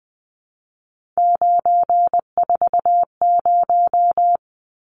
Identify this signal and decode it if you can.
Morse code, heard as a keyed tone, about 20 wpm, 700 Hz. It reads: 940
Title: Numbers - 3-Digit 20wpm